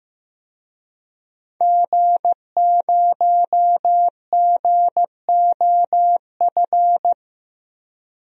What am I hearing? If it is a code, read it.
Morse code, heard as a keyed tone, about 15 wpm, 700 Hz. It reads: G0GOF
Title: DXCC-100 Call Signs - 5 Characters - Repeated 2x - 15wpm